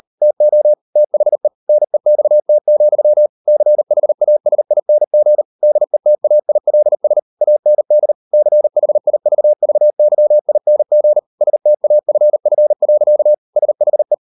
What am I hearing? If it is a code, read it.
Morse code, heard as a keyed tone, about 40 wpm, 600 Hz. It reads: TO THE NEXT, CHASING DETAILS AND CHIVVYING STAFF. SHE